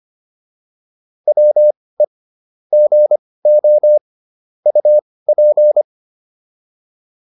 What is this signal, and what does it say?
Morse code, heard as a keyed tone, about 25 wpm, spaced as for 18 wpm, 600 Hz. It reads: WE GO UP